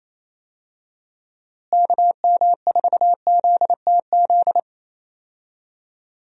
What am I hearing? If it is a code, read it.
Morse code, heard as a keyed tone, about 28 wpm, 700 Hz. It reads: KM4ZTZ
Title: USA Call Signs - 6 Characters - Repeated 3x - 28wpm